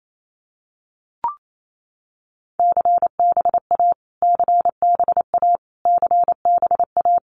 Morse code, heard as a keyed tone, about 28 wpm, 700 Hz. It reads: CBA CBA CBA
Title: Abbreviations Repeated 3x - 28wpm